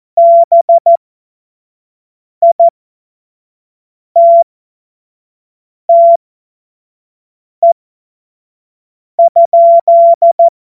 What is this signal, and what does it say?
Morse code, heard as a keyed tone, about 14 wpm, spaced as for 5 wpm, 680 Hz. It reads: BITTE?